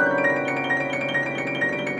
There is no code.